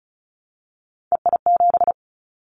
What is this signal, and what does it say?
Morse code, heard as a keyed tone, about 36 wpm, 700 Hz. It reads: EI7
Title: Callsign Prefixes for Top 50 DXCC - 35wpm